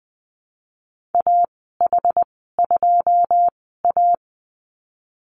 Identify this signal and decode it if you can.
Morse code, heard as a keyed tone, about 20 wpm, spaced as for 15 wpm, 700 Hz. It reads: AH2A
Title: 088 Lesson 15g - Number 5 - Call Signs 20wpm